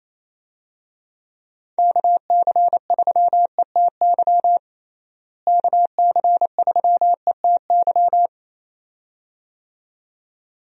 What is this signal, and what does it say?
Morse code, heard as a keyed tone, about 28 wpm, 700 Hz. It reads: KC3ETY KC3ETY
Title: USA Call Signs - 6 Characters - Repeated 2x - 28wpm